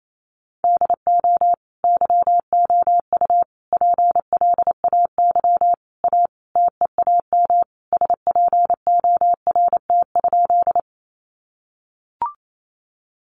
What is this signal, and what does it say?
Morse code, heard as a keyed tone, about 28 wpm, 700 Hz. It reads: DO YOU PLAY A TEAM SPORT?
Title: Sentences from Top 1500 Words 28wpm